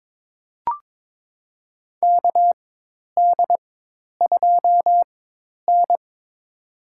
Morse code, heard as a keyed tone, about 22 wpm, spaced as for 10 wpm, 700 Hz. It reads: KD2N
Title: USA Call Signs - 4x Character Spacing - 22wpm